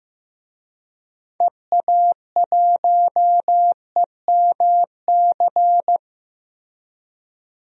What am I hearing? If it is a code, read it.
Morse code, heard as a keyed tone, about 16 wpm, 700 Hz. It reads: EA1EMC
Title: DXCC-100 Call Signs - 6 Characters - Repeated 3x - 15wpm